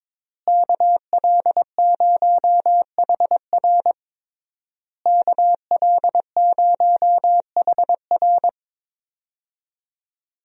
Morse code, heard as a keyed tone, about 22 wpm, 700 Hz. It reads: KL0HR KL0HR
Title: USA Call Signs - 5 Characters - Repeated 3x - 22wpm